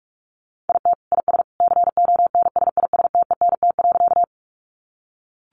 Morse code, heard as a keyed tone, about 44 wpm, 700 Hz. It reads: IT IS CONSISTENT.
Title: Sentences from Top 1500 Words 45wpm